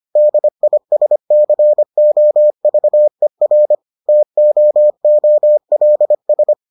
Morse code, heard as a keyed tone, about 25 wpm, 600 Hz. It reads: DISCOVER TOOLS